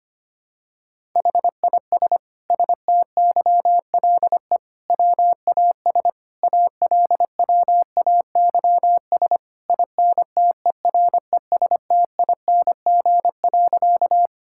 Morse code, heard as a keyed tone, about 25 wpm, 700 Hz. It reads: HIS STYLE WAS ALWAYS INTERESTING.
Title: Sentences from Top 1000 Words - Review Part 5 - 25wpm